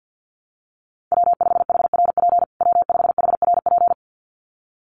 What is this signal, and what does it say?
Morse code, heard as a keyed tone, about 50 wpm, 700 Hz. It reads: W5HRP W5HRP
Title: USA Call Signs Repeated 2x 50wpm